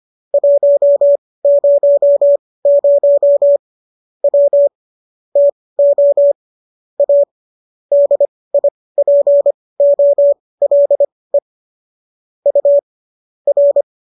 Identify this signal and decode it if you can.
Morse code, heard as a keyed tone, about 25 wpm, spaced as for 18 wpm, 570 Hz. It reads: 100 W TO A DIPOLE U R